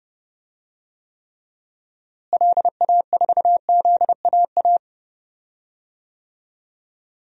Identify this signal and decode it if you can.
Morse code, heard as a keyed tone, about 30 wpm, 700 Hz. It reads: LA4ZAA
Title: DXCC-100 Call Signs - 6 Characters - Repeated 2x - 30wpm